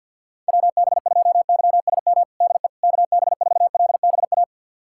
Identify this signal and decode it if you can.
Morse code, heard as a keyed tone, about 52 wpm, 700 Hz. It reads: WB2YSM DE KB4LBA
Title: USA Callsign DE Callsign 50wpm